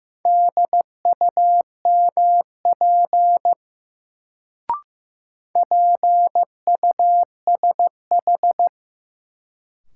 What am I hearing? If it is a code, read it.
Morse code, heard as a keyed tone, about 15 wpm, 700 Hz. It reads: DUMP PUSH
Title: Sentences from Top 1300 Words - New Word Drill Alt - 15wpm